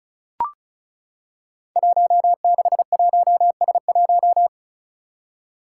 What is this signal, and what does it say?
Morse code, heard as a keyed tone, about 34 wpm, 700 Hz. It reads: 161S1
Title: Five Challenging Letter-Number Combinations 35wpm